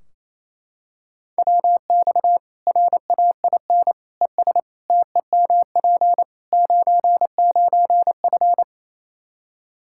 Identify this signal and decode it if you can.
Morse code, heard as a keyed tone, about 28 wpm, 700 Hz. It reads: WX RAIN ES TEMP 99F
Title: Practice QSO - Exchange 2 - Info - 28wpm